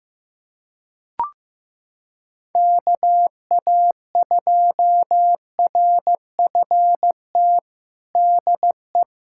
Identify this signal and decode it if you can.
Morse code, heard as a keyed tone, about 16 wpm, 700 Hz. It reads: KA2RFT DE
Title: USA Callsign DE Callsign 15wpm